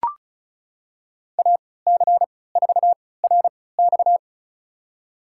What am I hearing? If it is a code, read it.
Morse code, heard as a keyed tone, about 36 wpm, spaced as for 20 wpm, 700 Hz. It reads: AC4RX
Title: USA Call Signs - 3x Character Spacing - 35wpm